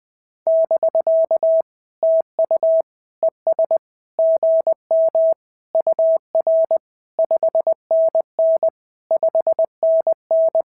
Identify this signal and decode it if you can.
Morse code, heard as a keyed tone, about 20 wpm, 650 Hz. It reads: <BK> TU ES GM UR 5NN 5NN
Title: SOTA QSO Practice 20wpm